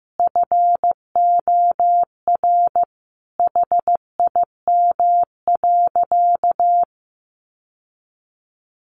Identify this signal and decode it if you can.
Morse code, heard as a keyed tone, about 15 wpm, 700 Hz. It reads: FOR HIM.
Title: Sentences from Top 600 Words 15wpm